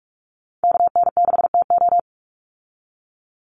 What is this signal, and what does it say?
Morse code, heard as a keyed tone, about 44 wpm, 700 Hz. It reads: KN6TO
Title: USA Call Signs 45wpm